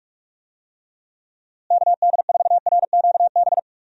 Morse code, heard as a keyed tone, about 41 wpm, 700 Hz. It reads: KD4RQB
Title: USA Call Signs - 6 Characters - Repeated 3x - 45wpm